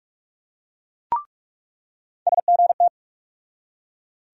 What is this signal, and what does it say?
Morse code, heard as a keyed tone, about 43 wpm, 700 Hz. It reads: SGT
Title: USA Call Sign Postfixes - 3 Letters 45wpm